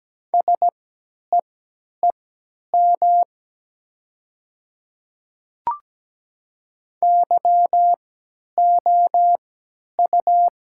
Morse code, heard as a keyed tone, about 17 wpm, spaced as for 10 wpm, 700 Hz. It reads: SEEM YOU
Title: 3x Character Spacing - Top 300 Words - 17wpm